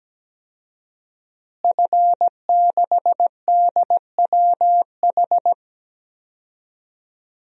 Morse code, heard as a keyed tone, about 17 wpm, 700 Hz. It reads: F6DWH
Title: DXCC-100 Call Signs - 5 Characters - Repeated 2x - 17wpm